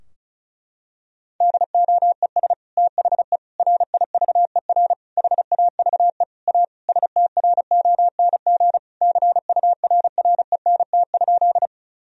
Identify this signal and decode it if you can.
Morse code, heard as a keyed tone, about 34 wpm, 700 Hz. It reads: DOES THE RIVER HAVE A STRONG CURRENT?
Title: Sentences from Top 1000 Words - Review Part 4 - 35wpm